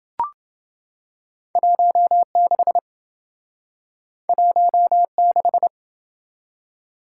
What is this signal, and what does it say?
Morse code, heard as a keyed tone, about 27 wpm, 700 Hz. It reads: 16 16